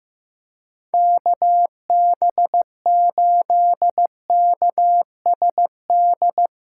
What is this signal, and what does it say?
Morse code, heard as a keyed tone, about 15 wpm, 700 Hz. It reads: KB8KSD
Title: USA Callsign DE Callsign 15wpm